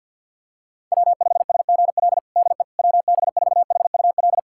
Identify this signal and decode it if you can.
Morse code, heard as a keyed tone, about 52 wpm, 700 Hz. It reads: W5SGL DE WB4HUL
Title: USA Callsign DE Callsign 50wpm